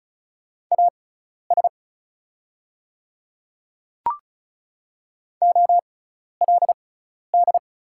Morse code, heard as a keyed tone, about 34 wpm, spaced as for 12 wpm, 700 Hz. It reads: AS OLD
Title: 6x Character Spacing - Top 300 Words - 35wpm